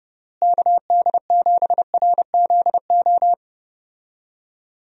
Morse code, heard as a keyed tone, about 32 wpm, 700 Hz. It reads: KD7RZO
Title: USA Call Signs - 6 Characters - Repeated 3x - 30wpm